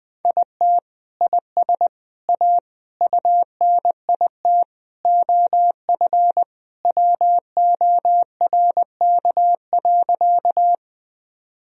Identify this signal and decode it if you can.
Morse code, heard as a keyed tone, about 21 wpm, 700 Hz. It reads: IT IS A UNIT OF WORK.
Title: Sentences from Top 1000 Words - Review Part 2 - 20wpm